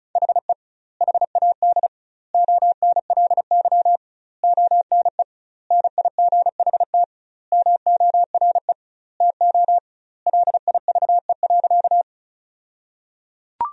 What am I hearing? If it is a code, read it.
Morse code, heard as a keyed tone, about 34 wpm, 700 Hz. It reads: HE HAD ONLY ONE NIGHT MORE TO LIVE.
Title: 2x Word-Spacing - Sentences from Top 300 Words - 35wpm